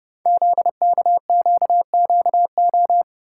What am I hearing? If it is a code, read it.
Morse code, heard as a keyed tone, about 30 wpm, 700 Hz. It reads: ZKQQO
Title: Code Groups - 5 Letters - 30wpm